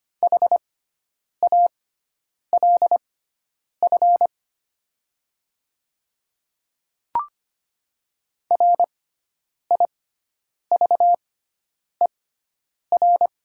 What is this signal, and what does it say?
Morse code, heard as a keyed tone, about 25 wpm, spaced as for 9 wpm, 700 Hz. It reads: HALF RIVER
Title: 6x Character Spacing - Top 500 Words - 25wpm